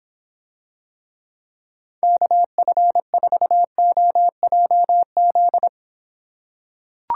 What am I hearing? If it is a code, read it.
Morse code, heard as a keyed tone, about 26 wpm, 700 Hz. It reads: KF4OJZ